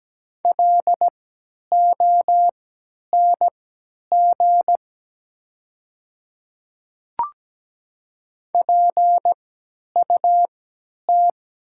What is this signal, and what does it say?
Morse code, heard as a keyed tone, about 17 wpm, spaced as for 10 wpm, 700 Hz. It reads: LONG PUT
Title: 3x Character Spacing - Top 300 Words - 17wpm